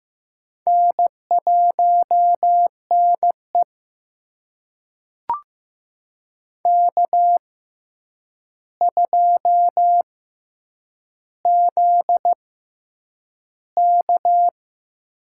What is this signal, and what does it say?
Morse code, heard as a keyed tone, about 15 wpm, spaced as for 8 wpm, 700 Hz. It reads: N1NE K2ZK
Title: USA Call Signs - 4 Characters - 6x Character Spacing - 15wpm